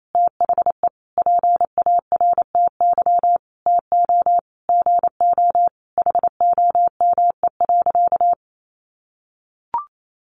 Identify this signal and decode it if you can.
Morse code, heard as a keyed tone, about 28 wpm, 700 Hz. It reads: THE PARTY TO GO HOME.
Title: Sentences from Top 1400 Words 28wpm